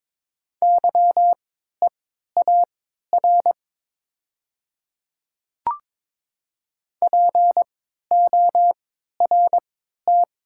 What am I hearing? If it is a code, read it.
Morse code, heard as a keyed tone, about 22 wpm, spaced as for 12 wpm, 700 Hz. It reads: YEAR PORT
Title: 3x Character Spacing - Top 200 Words - 22wpm